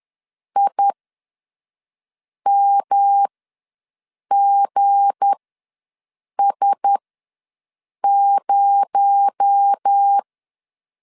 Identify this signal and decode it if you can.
Morse code, heard as a keyed tone, about 10 wpm, spaced as for 5 wpm, 800 Hz. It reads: IMGS0
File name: LEZIONI / 8 YJ45 / 8 YJ ws 10wpm 4043